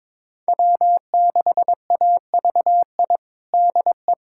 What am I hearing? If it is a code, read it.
Morse code, heard as a keyed tone, about 22 wpm, 700 Hz. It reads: W6AVI DE
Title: USA Callsign DE Callsign - 5 Characters - 22wpm